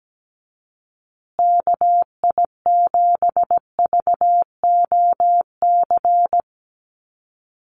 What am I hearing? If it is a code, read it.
Morse code, heard as a keyed tone, about 17 wpm, 700 Hz. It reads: KI7VOC
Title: USA Call Signs Repeated 2x 17wpm